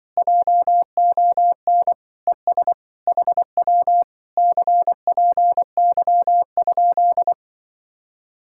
Morse code, heard as a keyed tone, about 25 wpm, 700 Hz. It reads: JON ES HW CPY?
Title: Speed-Racing Common QSO Phrases - 20wpm